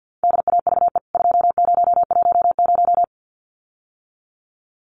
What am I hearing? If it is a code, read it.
Morse code, heard as a keyed tone, about 50 wpm, 700 Hz. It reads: DAVE 2010
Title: CWT Exchange 50wpm